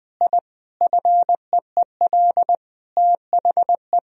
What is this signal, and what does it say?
Morse code, heard as a keyed tone, about 20 wpm, 700 Hz. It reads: I FEEL THE
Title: Sentences from Top 1000 Words - Review Part 1 - 20wpm